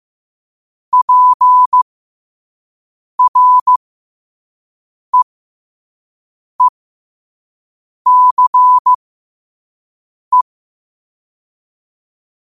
Morse code, heard as a keyed tone, about 15 wpm, spaced as for 5 wpm, 1000 Hz. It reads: PREECE
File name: qotdfiles / 5wpm